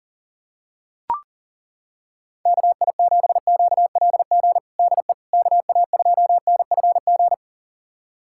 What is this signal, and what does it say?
Morse code, heard as a keyed tone, about 40 wpm, 700 Hz. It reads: KI7QLG DE KA2NFG
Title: USA Callsign DE Callsign - 6 Characters - 40wpm